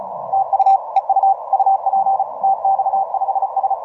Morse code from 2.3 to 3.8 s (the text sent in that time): TX5S